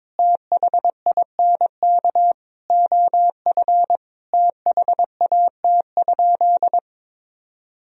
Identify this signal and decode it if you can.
Morse code, heard as a keyed tone, about 22 wpm, 700 Hz. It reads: THINK OF THAT?
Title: Sentences from Top 200 Words and Abbreviations 22wpm